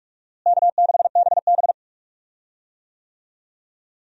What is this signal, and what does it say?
Morse code, heard as a keyed tone, about 43 wpm, 700 Hz. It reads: K6BB
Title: USA Call Signs - 4 Characters - Repeated 3x - 45wpm